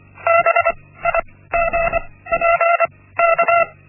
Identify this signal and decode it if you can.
Morse code, heard as a keyed tone, about 24 wpm, spaced as for 16 wpm, 680 Hz. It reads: BIGPK
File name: Linksmakalnis-1